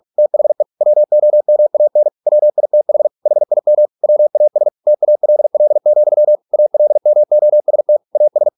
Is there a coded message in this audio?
Yes: Morse code from 0.2 to 8.5 s, heard as a keyed tone, about 46 wpm, 600 Hz. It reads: THE WOMAN WITH HIM WAS TALL, ALMOST AS